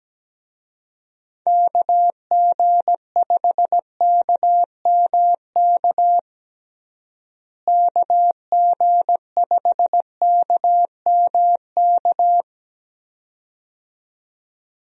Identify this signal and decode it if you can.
Morse code, heard as a keyed tone, about 17 wpm, 700 Hz. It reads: KG5KMK KG5KMK
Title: USA Call Signs - 6 Characters - Repeated 2x - 17wpm